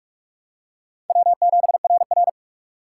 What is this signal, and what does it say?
Morse code, heard as a keyed tone, about 47 wpm, 700 Hz. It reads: W7RR
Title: USA Call Signs - 4 Characters - 3x Character Spacing - 45wpm